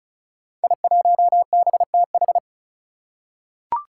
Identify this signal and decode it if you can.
Morse code, heard as a keyed tone, about 36 wpm, 700 Hz. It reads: I1BTH
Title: Five Challenging Letter-Number Combinations 35wpm